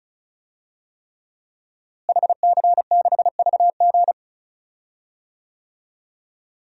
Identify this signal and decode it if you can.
Morse code, heard as a keyed tone, about 34 wpm, 700 Hz. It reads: HC6VG
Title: DXCC-100 Call Signs - 5 Characters - Repeated 3x - 35wpm